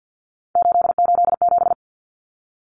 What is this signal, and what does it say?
Morse code, heard as a keyed tone, about 50 wpm, 700 Hz. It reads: 887
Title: Common Cut Numbers - 3-Digit 50wpm